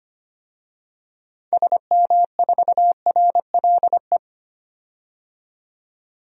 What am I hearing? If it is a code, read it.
Morse code, heard as a keyed tone, about 25 wpm, 700 Hz. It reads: SM4RLE
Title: DXCC-100 Call Signs - 6 Characters - 25wpm